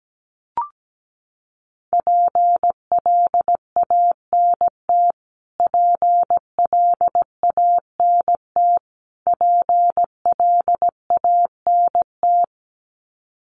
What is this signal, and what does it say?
Morse code, heard as a keyed tone, about 17 wpm, 700 Hz. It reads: PLANT PLANT PLANT
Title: Top 500 Words Repeated 17wpm